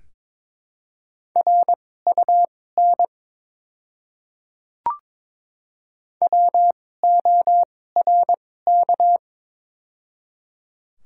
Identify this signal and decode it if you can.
Morse code, heard as a keyed tone, about 22 wpm, spaced as for 16 wpm, 700 Hz. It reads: RUN WORK
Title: 2x Character Spacing - Top 300 Words - 22wpm